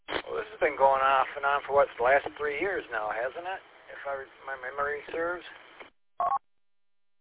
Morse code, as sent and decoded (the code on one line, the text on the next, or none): none